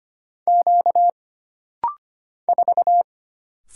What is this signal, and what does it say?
Morse code, heard as a keyed tone, about 25 wpm, 700 Hz. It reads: Q 4
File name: Single Character (Letter:Number:Punctuation:Prosign) - Rapid-Fire - 25wpm